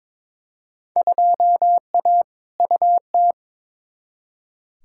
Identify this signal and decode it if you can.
Morse code, heard as a keyed tone, about 21 wpm, 700 Hz. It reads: 2A UT
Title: Field Day Exchanges - 22wpm